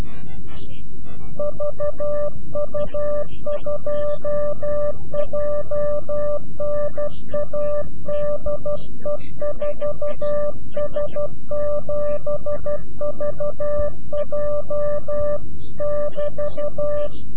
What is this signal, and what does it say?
Morse code, heard as a keyed tone, about 12 wpm, 590 Hz. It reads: VU2JNA DE 4S7VJ <BT>
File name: CW-QSO / CQ Call / CQ-QSO